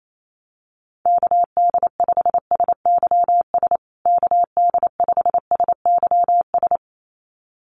Morse code, heard as a keyed tone, about 28 wpm, 700 Hz. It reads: KD5SYS KD5SYS
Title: USA Call Signs Repeated 2x 28wpm